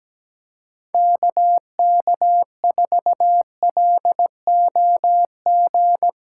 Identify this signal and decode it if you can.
Morse code, heard as a keyed tone, about 17 wpm, 700 Hz. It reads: KK4LOG
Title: USA Callsign DE Callsign - 6 Characters - 17wpm